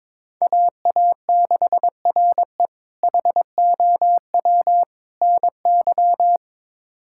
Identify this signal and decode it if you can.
Morse code, heard as a keyed tone, about 21 wpm, 700 Hz. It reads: AA6RE HOW NY
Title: K1USN SST QSO Practice - 22wpm